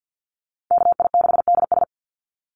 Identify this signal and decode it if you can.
Morse code, heard as a keyed tone, about 50 wpm, 700 Hz. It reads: KI6DS
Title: NA SOTA Chaser Callsigns 50wpm